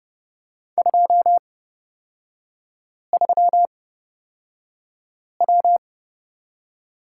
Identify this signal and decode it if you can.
Morse code, heard as a keyed tone, about 30 wpm, spaced as for 5 wpm, 700 Hz. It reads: 23W
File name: Lesson 37e - Letter x - Three Characters 30wpm